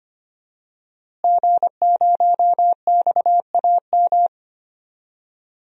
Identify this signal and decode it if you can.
Morse code, heard as a keyed tone, about 25 wpm, 700 Hz. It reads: G0XAM
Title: DXCC-100 Call Signs - 5 Characters - Repeated 3x - 25wpm